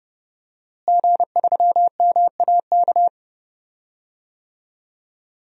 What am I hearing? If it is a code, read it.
Morse code, heard as a keyed tone, about 30 wpm, 700 Hz. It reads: G3MAK
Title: DXCC-100 Call Signs - 5 Characters - Repeated 2x - 30wpm